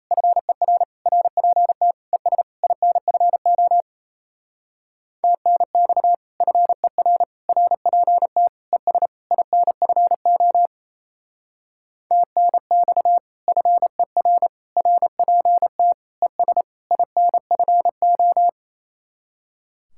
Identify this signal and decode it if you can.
Morse code, heard as a keyed tone, about 34 wpm, 700 Hz. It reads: FER RPT ES INFO TNX FER RPT ES INFO TNX FER RPT ES INFO
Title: Speed-Racing Common QSO Phrases - 28wpm